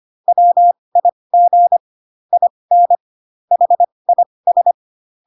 Morse code, heard as a keyed tone, about 25 wpm, spaced as for 20 wpm, 700 Hz. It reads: WIG IN HIS
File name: Pinocchio / Pinocchio ch2 15 25 ik1hge